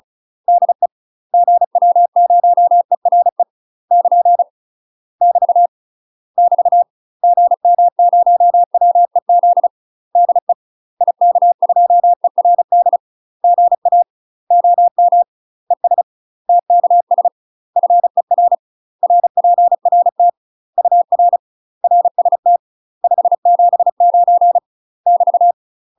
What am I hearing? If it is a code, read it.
Morse code, heard as a keyed tone, about 34 wpm, 700 Hz. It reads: DE GW0ERE <KN> <BT> <BT> GM0WEZ DE IK2ERD GA OM ES TKS FER RPRT UR RST 579 <BT>
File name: UK MASTER QSOs TRAINING / 35WPM / UK MASTER QSOs 0041